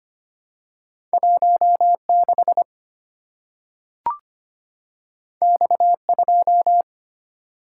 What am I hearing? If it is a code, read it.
Morse code, heard as a keyed tone, about 25 wpm, 700 Hz. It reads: 16 X2
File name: Two Letter-Number Combinations - 25wpm